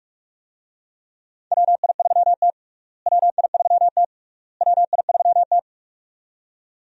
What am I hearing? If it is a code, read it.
Morse code, heard as a keyed tone, about 41 wpm, 700 Hz. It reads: WI3T WI3T WI3T
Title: USA Call Signs - 4 Characters - Repeated 3x - 45wpm